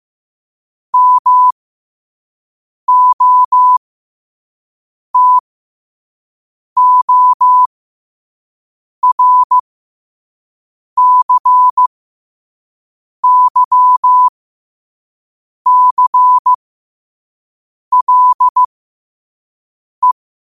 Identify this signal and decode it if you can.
Morse code, heard as a keyed tone, about 15 wpm, spaced as for 5 wpm, 1000 Hz. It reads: MOTORCYCLE